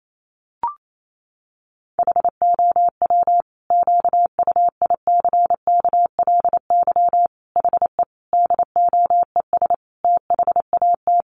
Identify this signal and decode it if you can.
Morse code, heard as a keyed tone, about 28 wpm, 700 Hz. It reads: HOW QUICKLY HE DOES THAT
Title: Sentences from Top 700 Words 28wpm